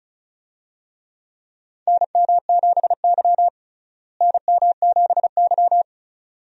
Morse code, heard as a keyed tone, about 36 wpm, 700 Hz. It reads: NM7Y NM7Y
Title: USA Call Signs - 4 Characters - Repeated 2x - 35wpm